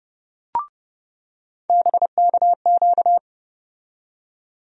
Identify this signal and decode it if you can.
Morse code, heard as a keyed tone, about 30 wpm, 700 Hz. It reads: BKQ
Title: USA Call Sign Postfixes - 3 Letters 30wpm